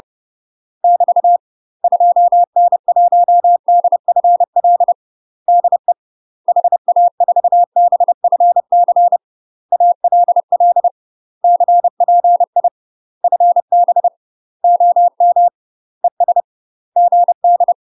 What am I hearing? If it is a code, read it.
Morse code, heard as a keyed tone, about 30 wpm, 700 Hz. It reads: <BT> 2N1DFL DE HA4BFC ALL CPI FB OM ES GD
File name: UK MASTER QSOs TRAINING/30WPM